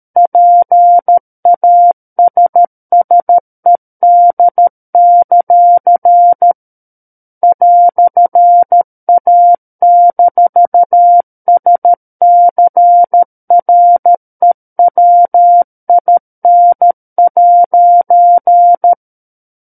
Morse code, heard as a keyed tone, about 13 wpm, 700 Hz. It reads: PASSED; "A-SCREWIN'
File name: Book One / 04-Chapter 4 - 13 WPM